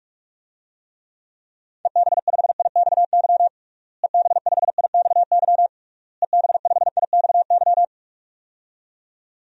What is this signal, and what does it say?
Morse code, heard as a keyed tone, about 46 wpm, 700 Hz. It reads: EB5IXY EB5IXY EB5IXY
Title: DXCC-100 Call Signs - 6 Characters - Repeated 3x - 45wpm